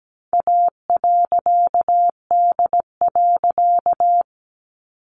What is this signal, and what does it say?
Morse code, heard as a keyed tone, about 17 wpm, 700 Hz. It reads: A.D.
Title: A Sound Of Thunder 17wpm